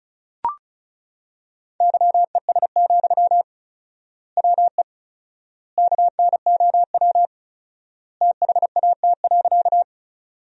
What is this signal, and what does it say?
Morse code, heard as a keyed tone, about 34 wpm, 700 Hz. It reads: YES, WE KNOW THAT.
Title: 4x Word-Spacing - Sentences from Top 500 Words - 35wpm